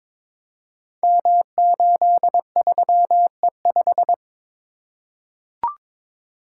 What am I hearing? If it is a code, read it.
Morse code, heard as a keyed tone, about 22 wpm, 700 Hz. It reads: M83E5
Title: Code Groups - 5 Letter:Numbers - 22wpm